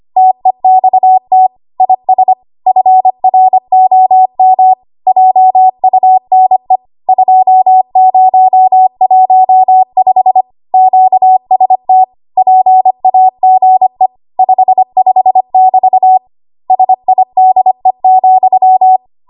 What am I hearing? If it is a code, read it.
Morse code, heard as a keyed tone, about 25 wpm, 750 Hz. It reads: TEXT IS FROM JUNE 2015 QST PAGE 55<BT> SIDE,
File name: Archive / 25 WPM / 25WPM